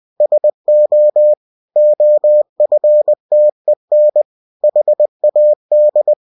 Morse code, heard as a keyed tone, about 20 wpm, 600 Hz. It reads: SO OFTEN HAD
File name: cw-books / winnie-the-pooh / winnie-20wpm-page-0091